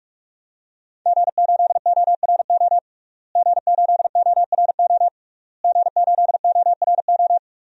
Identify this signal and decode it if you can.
Morse code, heard as a keyed tone, about 46 wpm, 700 Hz. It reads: G8ORO G8ORO G8ORO
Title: DXCC-100 Call Signs - 5 Characters - Repeated 3x - 45wpm